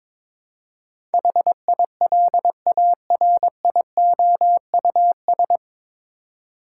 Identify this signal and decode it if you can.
Morse code, heard as a keyed tone, about 22 wpm, 700 Hz. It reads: HILARIOUS
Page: Morse Code Ninja practice 22wpm